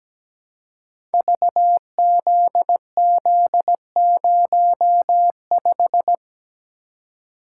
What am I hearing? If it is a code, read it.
Morse code, heard as a keyed tone, about 17 wpm, 700 Hz. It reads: VZZ05
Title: Code Groups - 5 Letter:Numbers - 17wpm